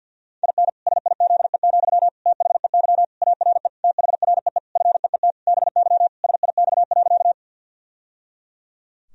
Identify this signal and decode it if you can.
Morse code, heard as a keyed tone, about 52 wpm, 700 Hz. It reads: IN SIZE, THEY ARE THREE FEET BY SIX.
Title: Sentences from Top 1000 Words - Review Part 2 - 50wpm